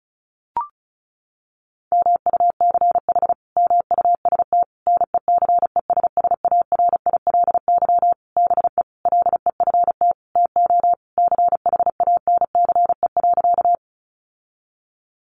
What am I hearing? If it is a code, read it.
Morse code, heard as a keyed tone, about 34 wpm, 700 Hz. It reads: MUCH MUST NECESSARILY BE LEFT TO CHANCE.
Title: Sentences from Top 1400 Words 35wpm